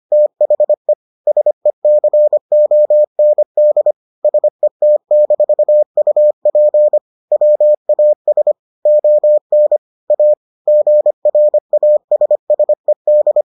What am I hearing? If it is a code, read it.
Morse code, heard as a keyed tone, about 25 wpm, 600 Hz. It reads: THE SECOND SET-UP WAS ON A GRASSED